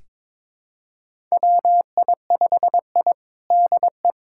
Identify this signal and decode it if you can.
Morse code, heard as a keyed tone, about 21 wpm, 700 Hz. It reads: WI5I DE
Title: USA Callsign DE Callsign - 4 Characters - 22wpm